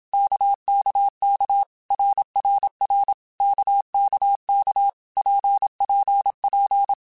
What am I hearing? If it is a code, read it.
Morse code, heard as a keyed tone, about 27 wpm, 800 Hz. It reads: KKK RRR KKK PPP
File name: Basic Letter Groups By7Groups / RKPX / RKPX@25